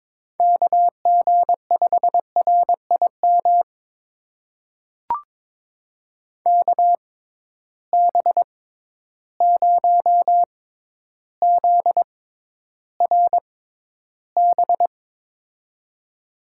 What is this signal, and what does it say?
Morse code, heard as a keyed tone, about 22 wpm, spaced as for 11 wpm, 700 Hz. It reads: KG5RIM KB0ZRB
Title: USA Call Signs - 6 Characters - 6x Character Spacing - 22wpm